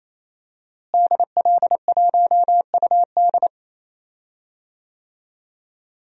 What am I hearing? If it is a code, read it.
Morse code, heard as a keyed tone, about 27 wpm, 700 Hz. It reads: DL1UD
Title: DXCC-100 Call Signs - 5 Characters - Repeated 3x - 28wpm